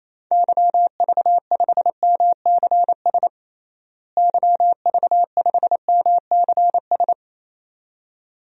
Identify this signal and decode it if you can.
Morse code, heard as a keyed tone, about 28 wpm, 700 Hz. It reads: YV5MCS YV5MCS
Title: DXCC-100 Call Signs - 6 Characters - Repeated 2x - 28wpm